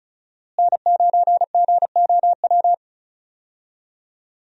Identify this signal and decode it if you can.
Morse code, heard as a keyed tone, about 34 wpm, 700 Hz. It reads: N9GOW
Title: USA Call Signs - 5 Characters - Repeated 2x - 35wpm